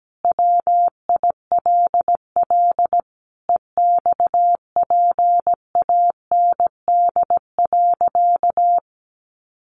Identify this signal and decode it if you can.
Morse code, heard as a keyed tone, about 17 wpm, 700 Hz. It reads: WILL EXPAND.
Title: Sentences from Top 1400 Words 17wpm